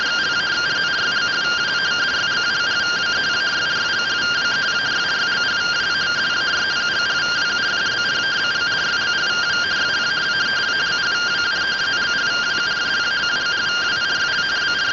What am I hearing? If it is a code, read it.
Two-tone teleprinter signal, about 170 Hz shift.